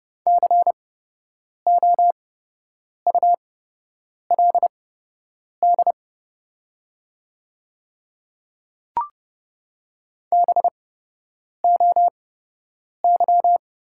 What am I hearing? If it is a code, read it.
Morse code, heard as a keyed tone, about 30 wpm, spaced as for 8 wpm, 700 Hz. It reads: COULD BOY
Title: 8x Character Spacing - Top 300 Words - 30wpm